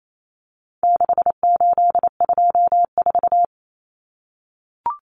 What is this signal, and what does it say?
Morse code, heard as a keyed tone, about 28 wpm, 700 Hz. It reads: 6824
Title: Common Cut Numbers - 4-Digit 28wpm